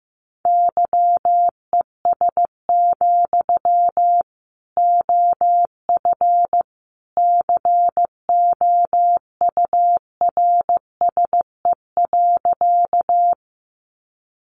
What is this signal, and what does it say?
Morse code, heard as a keyed tone, about 15 wpm, 700 Hz. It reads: YES, OF COURSE.
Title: Sentences from Top 500 Words 15wpm